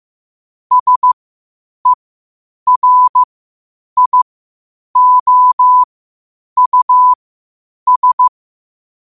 Morse code, heard as a keyed tone, about 15 wpm, spaced as for 8 wpm, 1000 Hz. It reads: SERIOUS
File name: qotdfiles / 8wpm